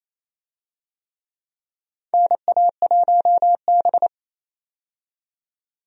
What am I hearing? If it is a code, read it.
Morse code, heard as a keyed tone, about 28 wpm, 700 Hz. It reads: NA1B
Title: USA Call Signs - 4 Characters - 28wpm